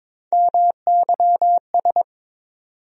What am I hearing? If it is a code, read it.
Morse code, heard as a keyed tone, about 22 wpm, 700 Hz. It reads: MYS
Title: USA Call Sign Postfixes - 3 Letters 22wpm